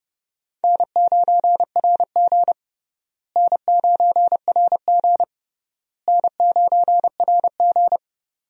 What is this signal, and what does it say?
Morse code, heard as a keyed tone, about 29 wpm, 700 Hz. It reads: N9RG N9RG N9RG
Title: USA Call Signs - 4 Characters - Repeated 3x - 30wpm